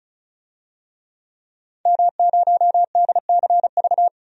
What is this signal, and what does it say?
Morse code, heard as a keyed tone, about 36 wpm, 700 Hz. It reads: M0DCV
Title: DXCC-100 Call Signs - 5 Characters - 35wpm